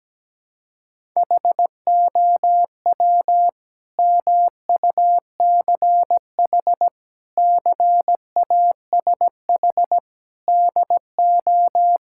Morse code, heard as a keyed tone, about 17 wpm, 700 Hz. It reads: HOW MUCH CASH DO
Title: Sentences from Top 1000 Words - Review Part 5 - 17wpm